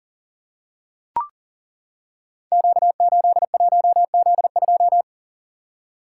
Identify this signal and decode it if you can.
Morse code, heard as a keyed tone, about 38 wpm, 700 Hz. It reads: Q81Z2
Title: Code Groups - 5 Letter:Numbers - 40wpm